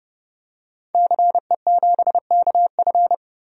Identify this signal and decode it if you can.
Morse code, heard as a keyed tone, about 30 wpm, 700 Hz. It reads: CE7KF
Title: DXCC-100 Call Signs - 5 Characters - Repeated 3x - 30wpm